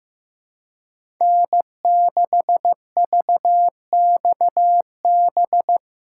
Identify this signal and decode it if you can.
Morse code, heard as a keyed tone, about 16 wpm, 700 Hz. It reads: N6VXB
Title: USA Callsign DE Callsign - 5 Characters - 15wpm